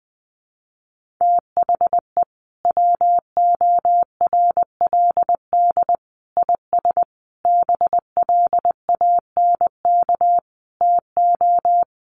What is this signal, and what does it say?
Morse code, heard as a keyed tone, about 20 wpm, 700 Hz. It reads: THE WORLD IS BLANK TO
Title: Sentences from Top 1500 Words 20wpm